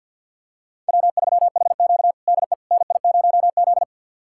Morse code, heard as a keyed tone, about 52 wpm, 690 Hz. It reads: W3HQ DE NI0Z
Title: USA Callsign DE Callsign - 4 Characters - 50wpm